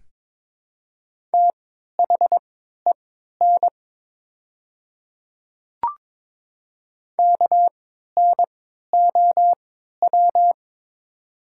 Morse code, heard as a keyed tone, about 23 wpm, spaced as for 13 wpm, 700 Hz. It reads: THEN KNOW
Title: 3x Character Spacing - Top 100 Words - 22wpm